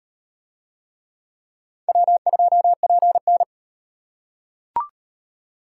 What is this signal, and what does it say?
Morse code, heard as a keyed tone, about 37 wpm, 700 Hz. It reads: W2PN